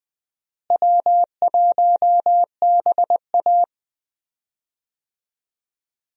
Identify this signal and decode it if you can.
Morse code, heard as a keyed tone, about 20 wpm, 700 Hz. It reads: W1BA
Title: USA Call Signs - 4 Characters - Repeated 2x - 20wpm